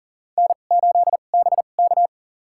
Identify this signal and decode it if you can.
Morse code, heard as a keyed tone, about 37 wpm, spaced as for 28 wpm, 700 Hz. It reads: N8BK
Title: USA Call Signs - 4 Characters - 2x Character Spacing - 40wpm